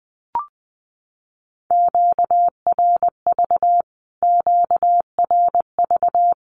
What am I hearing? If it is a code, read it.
Morse code, heard as a keyed tone, about 20 wpm, 700 Hz. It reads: QRV QRV
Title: Q-Codes Repeated 3x 20wpm